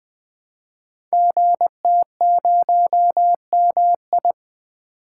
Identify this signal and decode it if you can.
Morse code, heard as a keyed tone, about 20 wpm, 700 Hz. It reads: GT0MI
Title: Code Groups - 5 Letter:Numbers - 20wpm